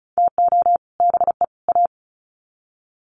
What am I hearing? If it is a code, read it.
Morse code, heard as a keyed tone, about 34 wpm, 700 Hz. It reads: TO BE A
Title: Sets of 3 Words 35wpm